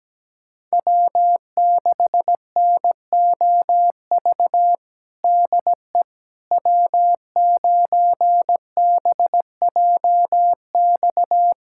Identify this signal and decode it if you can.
Morse code, heard as a keyed tone, about 17 wpm, 700 Hz. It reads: W6NOV DE W9BJX
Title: USA Callsign DE Callsign - 5 Characters - 17wpm